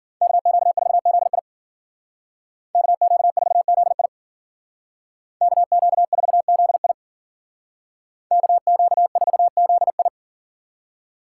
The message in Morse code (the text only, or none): KQ4ZI KQ4ZI KQ4ZI KQ4ZI